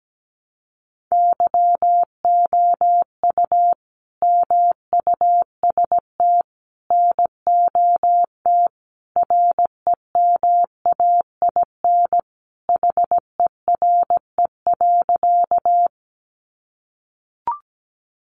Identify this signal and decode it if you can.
Morse code, heard as a keyed tone, about 17 wpm, 700 Hz. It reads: YOU MUST NOT REMAIN HERE.
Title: Sentences from Top 1200 Words 17wpm